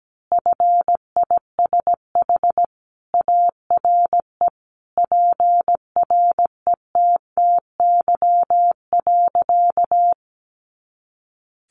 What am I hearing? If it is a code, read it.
Morse code, heard as a keyed tone, about 17 wpm, 700 Hz. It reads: FISH ARE PRETTY.
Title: Easy Phrases 17wpm